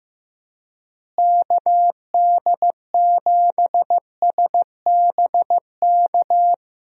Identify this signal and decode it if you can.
Morse code, heard as a keyed tone, about 15 wpm, 700 Hz. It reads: KD7SBK
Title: USA Call Signs - 6 Characters - Repeated 3x - 15wpm